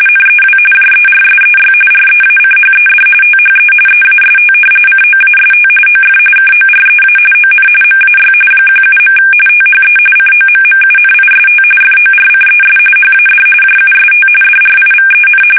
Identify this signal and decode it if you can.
RTTY, about 45.45 baud. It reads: NEW JERSEY REACHED THE EAST COAST OF KOREA ON MAY 17. THE COMMUNIST FORCES HAD JUST DRIVEN
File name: fsk850 / battleshipnj